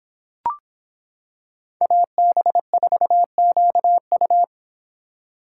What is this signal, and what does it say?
Morse code, heard as a keyed tone, about 25 wpm, 700 Hz. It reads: AB4QU